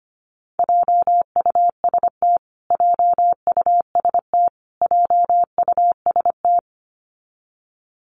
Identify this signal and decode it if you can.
Morse code, heard as a keyed tone, about 25 wpm, 700 Hz. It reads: JUST JUST JUST
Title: Top 500 Words Repeated 25wpm